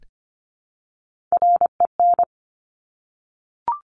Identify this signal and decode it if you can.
Morse code, heard as a keyed tone, about 25 wpm, 700 Hz. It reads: REN
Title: Top Three Letter Combinations in Words 25wpm